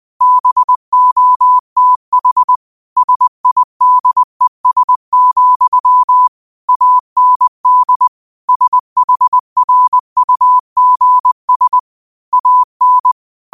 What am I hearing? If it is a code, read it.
Morse code, heard as a keyed tone, about 20 wpm, 1000 Hz. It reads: BOTH SIDES, AND SHRUGS AN
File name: qotdfiles / 20wpm